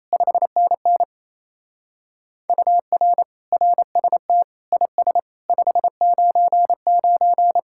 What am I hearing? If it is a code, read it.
Morse code, heard as a keyed tone, about 33 wpm, 700 Hz. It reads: 5NN UR RST IS 599
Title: Speed-Racing Common QSO Phrases - 28wpm